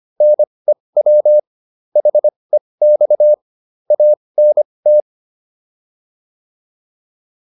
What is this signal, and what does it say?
Morse code, heard as a keyed tone, about 25 wpm, spaced as for 20 wpm, 600 Hz. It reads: NEW HEX ANT